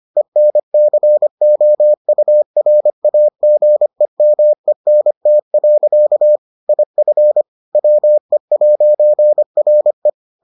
Morse code, heard as a keyed tone, about 25 wpm, 600 Hz. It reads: ENCOURAGEMENT. IF WE'RE